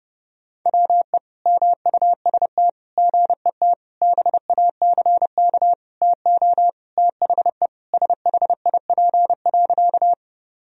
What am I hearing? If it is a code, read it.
Morse code, heard as a keyed tone, about 30 wpm, 700 Hz. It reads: WE MUST GET BACK TO THE SHIP.
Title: Sentences from Top 1000 Words - Review Part 2 - 30wpm